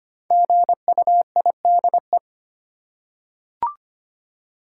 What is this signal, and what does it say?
Morse code, heard as a keyed tone, about 25 wpm, 700 Hz. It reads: GUIDE
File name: 5 Letter Words - First 3 Letters Given - 25wpm